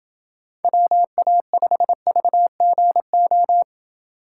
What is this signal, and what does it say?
Morse code, heard as a keyed tone, about 27 wpm, 700 Hz. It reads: WA5VGO